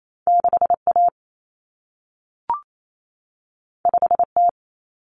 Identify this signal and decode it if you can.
Morse code, heard as a keyed tone, about 28 wpm, 700 Hz. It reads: 6A 5T
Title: Common Cut Numbers - 2-Digit 28wpm